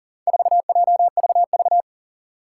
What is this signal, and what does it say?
Morse code, heard as a keyed tone, about 40 wpm, 700 Hz. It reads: 4JVV
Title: Four Challenging Letter-Number Combinations 40wpm